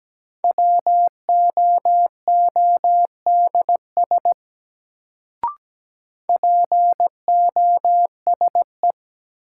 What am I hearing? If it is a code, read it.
Morse code, heard as a keyed tone, about 17 wpm, 700 Hz. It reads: WOODS POSE
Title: Sentences from Top 1000 Words - Review Part 2 - New Word Drill - 17wpm